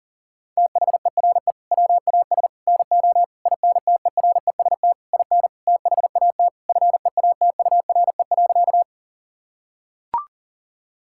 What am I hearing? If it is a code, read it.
Morse code, heard as a keyed tone, about 40 wpm, 700 Hz. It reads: THERE WAS NO INTEREST IN THAT FEATURE.
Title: Sentences from Top 1000 Words - Review Part 5 - 40wpm